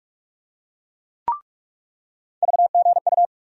Morse code, heard as a keyed tone, about 43 wpm, 700 Hz. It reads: VGU
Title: USA Call Sign Postfixes - 3 Letters 45wpm